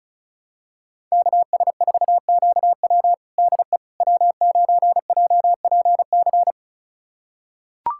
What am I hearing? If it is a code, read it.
Morse code, heard as a keyed tone, about 36 wpm, 700 Hz. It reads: KS4QW DE W9JPC
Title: USA Callsign DE Callsign - 5 Characters - 35wpm